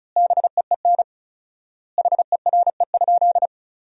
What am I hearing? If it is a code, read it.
Morse code, heard as a keyed tone, about 36 wpm, 700 Hz. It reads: BEEN HERE?
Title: 4x Word-Spacing - Sentences from Top 200 Words - 35wpm